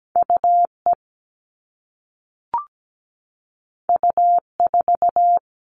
Morse code, heard as a keyed tone, about 17 wpm, 700 Hz. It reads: UE U4
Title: All Cut Numbers - 2-Digit 17wpm